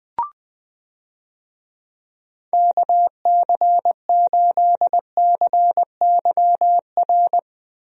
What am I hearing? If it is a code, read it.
Morse code, heard as a keyed tone, about 20 wpm, 700 Hz. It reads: KC8CYR
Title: USA Call Signs - 6 Characters - Repeated 3x - 20wpm